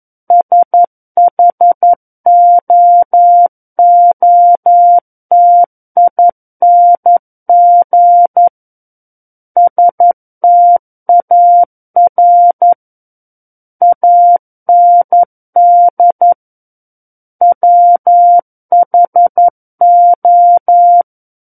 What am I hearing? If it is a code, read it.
Morse code, heard as a keyed tone, about 11 wpm, 700 Hz. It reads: SHOOTING STAR AND WHO
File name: 11 WPM